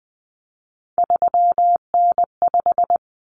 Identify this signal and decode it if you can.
Morse code, heard as a keyed tone, about 20 wpm, 700 Hz. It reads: 3N5
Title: Common Cut Numbers - 3-Digit 20wpm